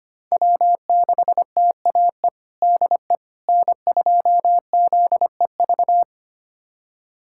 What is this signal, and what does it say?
Morse code, heard as a keyed tone, about 25 wpm, 700 Hz. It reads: W6TAE DE N2ZEV
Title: USA Callsign DE Callsign - 5 Characters - 25wpm